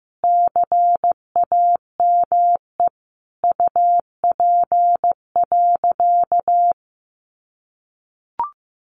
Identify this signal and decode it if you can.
Morse code, heard as a keyed tone, about 15 wpm, 700 Hz. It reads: CAME UP.
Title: Sentences from Top 1500 Words 15wpm